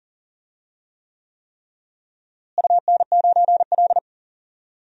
Morse code, heard as a keyed tone, about 40 wpm, 700 Hz. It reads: UN9L
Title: DXCC-100 Call Signs - 4 Characters - Repeated 3x - 40wpm